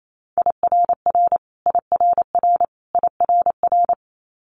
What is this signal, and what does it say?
Morse code, heard as a keyed tone, about 28 wpm, 700 Hz. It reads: IRR IRR IRR
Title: Multi-letter Phonemes Repeated 3x 28wpm